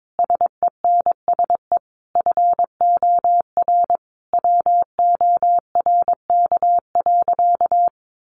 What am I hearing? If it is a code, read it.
Morse code, heard as a keyed tone, about 22 wpm, 700 Hz. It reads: SENSE FOR WORK.
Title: Sentences from Top 700 Words 22wpm